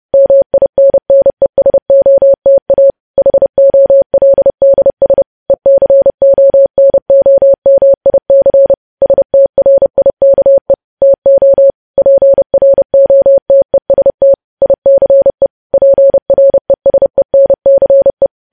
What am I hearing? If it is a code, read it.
Morse code, heard as a keyed tone, about 30 wpm, 550 Hz. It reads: MINNESOTA HOLDS ECONOMIC STRIKE TO PROTEST ICE PRESENCE